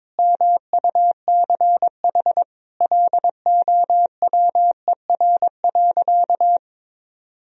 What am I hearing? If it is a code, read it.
Morse code, heard as a keyed tone, about 22 wpm, 700 Hz. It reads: MUCH LOWER.
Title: Sentences from Top 1000 Words - Review Part 4 - 22wpm